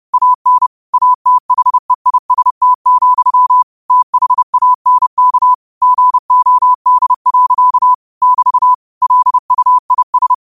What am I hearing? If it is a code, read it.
Morse code, heard as a keyed tone, about 30 wpm, 1000 Hz. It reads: AN ATHEIST, THANK GOD. <BT> LUIS
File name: qotdfiles / 30wpm